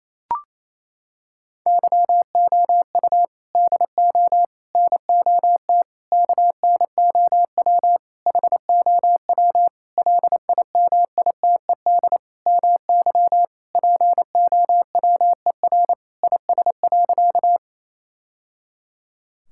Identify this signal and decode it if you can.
Morse code, heard as a keyed tone, about 27 wpm, 700 Hz. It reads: YOU DO NOT KNOW HOW LIMITED MY POWER IS.
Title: Sentences from Top 1000 Words - Review Part 6 - 28wpm